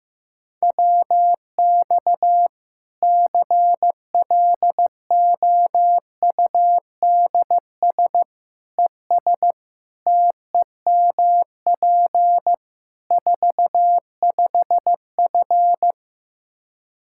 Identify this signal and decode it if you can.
Morse code, heard as a keyed tone, about 15 wpm, 700 Hz. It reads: WX CLOUDS ES TEMP 45F
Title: Practice QSO - Exchange 2 - Info - 15wpm